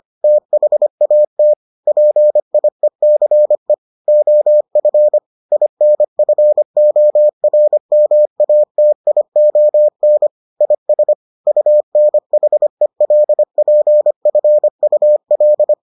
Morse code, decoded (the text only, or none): THAT PIECE OF INFORMATION IS UNHELPFUL